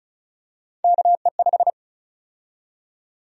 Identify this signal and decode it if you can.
Morse code, heard as a keyed tone, about 34 wpm, 700 Hz. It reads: KE5
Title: USA Call Sign Prefixes - 35wpm